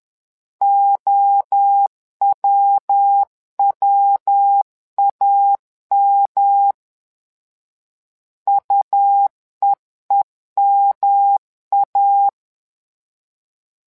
Morse code, heard as a keyed tone, about 10 wpm, 800 Hz. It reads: OWWAM UEEMA
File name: LEZIONI / 3 MWRO / 3 MWRO 10wpm 2148